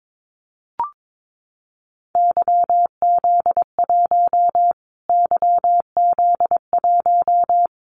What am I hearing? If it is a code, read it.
Morse code, heard as a keyed tone, about 22 wpm, 700 Hz. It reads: YZ1 YZ1
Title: Callsign Prefixes for Top 25 DXCC - Repeated 3x - 22wpm